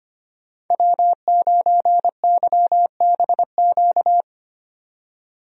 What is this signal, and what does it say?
Morse code, heard as a keyed tone, about 25 wpm, 700 Hz. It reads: W9YBQ
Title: USA Call Signs - 5 Characters - 25wpm